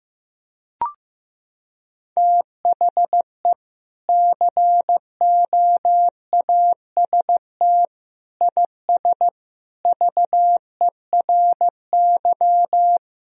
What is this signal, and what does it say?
Morse code, heard as a keyed tone, about 15 wpm, 700 Hz. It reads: THE COAST IS VERY
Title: Sentences from Top 1000 Words - Review Part 4 - 15wpm